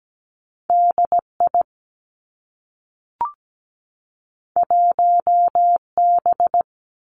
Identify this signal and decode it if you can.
Morse code, heard as a keyed tone, about 17 wpm, 700 Hz. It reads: DI 1B
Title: Two Challenging Letter-Number Combinations 17wpm